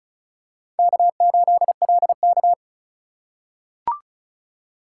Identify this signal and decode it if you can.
Morse code, heard as a keyed tone, about 34 wpm, 700 Hz. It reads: K8LK
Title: USA Call Signs - 4 Characters - 3x Character Spacing - 35wpm